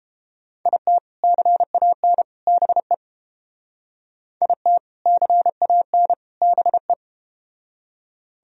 Morse code, heard as a keyed tone, about 30 wpm, 700 Hz. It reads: IT CAN BE IT CAN BE